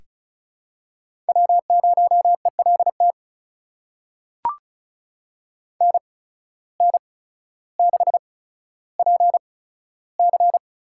Morse code, heard as a keyed tone, about 36 wpm, spaced as for 15 wpm, 700 Hz. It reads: W0ELT NN6PC
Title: USA Call Signs - 8x Character Spacing - 35wpm